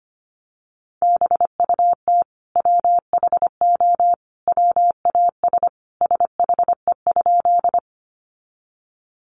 Morse code, heard as a keyed tone, about 25 wpm, 700 Hz. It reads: BUT WHO WAS SHE?
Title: Sentences from Top 100 Words 25wpm